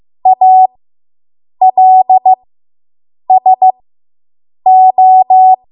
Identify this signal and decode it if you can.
Morse code, heard as a keyed tone, about 15 wpm, spaced as for 7 wpm, 750 Hz. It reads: ALSO